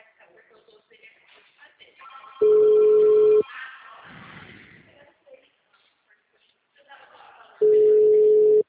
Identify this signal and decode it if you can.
Ringback tone (Europe, 425 Hz), with speech.